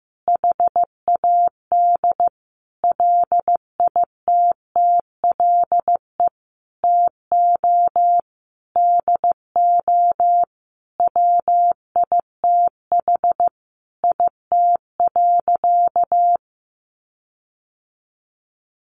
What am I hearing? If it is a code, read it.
Morse code, heard as a keyed tone, about 15 wpm, 700 Hz. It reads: HAD LITTLE TO DO WITH IT.
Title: Sentences from Top 1500 Words 15wpm